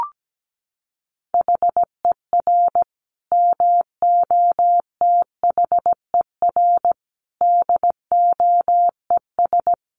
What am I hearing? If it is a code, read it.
Morse code, heard as a keyed tone, about 17 wpm, 700 Hz. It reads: HER MOTHER DOES
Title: Sentences from Top 1400 Words 17wpm